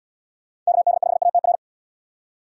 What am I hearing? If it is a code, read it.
Morse code, heard as a keyed tone, about 71 wpm, 700 Hz. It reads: KD5ATR